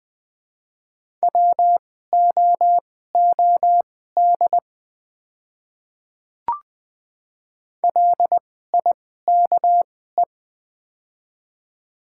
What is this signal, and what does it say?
Morse code, heard as a keyed tone, about 20 wpm, spaced as for 14 wpm, 700 Hz. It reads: WOOD LIKE
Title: 2x Character Spacing - Top 300 Words - 20wpm